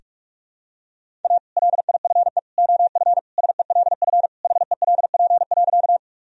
Morse code, heard as a keyed tone, about 41 wpm, 700 Hz. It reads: A LIFE OF SELF HELP.
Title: Sentences from Top 1000 Words - Review Part 1 - 45wpm